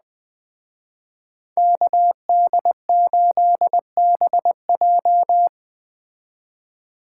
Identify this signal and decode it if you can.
Morse code, heard as a keyed tone, about 20 wpm, 700 Hz. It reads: KD8BJ
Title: Speed Racing - USA 2x2 Call Signs 17wpm